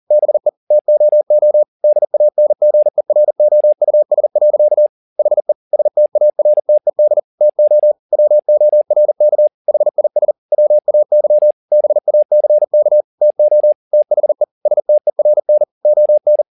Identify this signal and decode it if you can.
Morse code, heard as a keyed tone, about 40 wpm, 600 Hz. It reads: BE TOO DANGEROUS. HE STARTED TO WORK HIS WAY BACK TO THE STERN ON
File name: the old man and the sea 2 40wpm